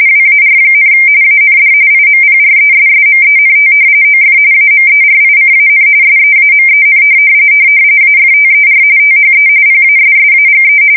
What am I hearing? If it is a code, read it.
RTTY, about 45.45 baud. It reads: WAR, AND IMPRESSMENT OF AMERICAN SEAMEN AFTER THE PASSAGE
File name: fsk170 / usnavy1